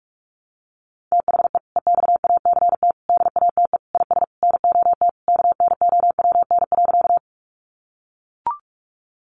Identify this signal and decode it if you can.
Morse code, heard as a keyed tone, about 44 wpm, 700 Hz. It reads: THE EXACT DATE IS NOT KNOWN.
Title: Sentences from Top 1300 Words 45wpm